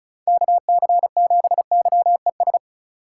Morse code, heard as a keyed tone, about 34 wpm, 700 Hz. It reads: KC7YES
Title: USA Call Signs - 4x Character Spacing - 35wpm